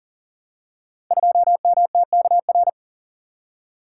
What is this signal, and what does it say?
Morse code, heard as a keyed tone, about 39 wpm, 700 Hz. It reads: 2MTKR
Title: Code Groups - 5 Letter:Numbers - 40wpm